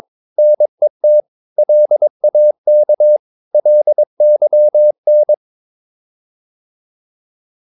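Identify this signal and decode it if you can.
Morse code, heard as a keyed tone, about 23 wpm, 600 Hz. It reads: NET LAK LYN
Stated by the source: Morse practice of short Danish words at 22 wpm, from CW med Gnister.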